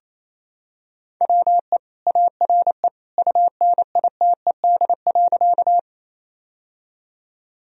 Morse code, heard as a keyed tone, about 28 wpm, 700 Hz. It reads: WE ARE UNITED.
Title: Sentences from Top 1000 Words - Review Part 4 - 28wpm